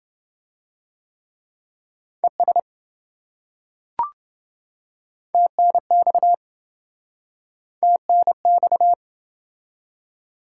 Morse code, heard as a keyed tone, about 26 wpm, 700 Hz. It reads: ES TNX TNX